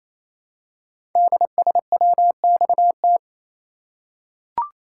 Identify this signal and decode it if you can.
Morse code, heard as a keyed tone, about 29 wpm, 700 Hz. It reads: DSWXT
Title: Code Groups - 5 Letters - 28wpm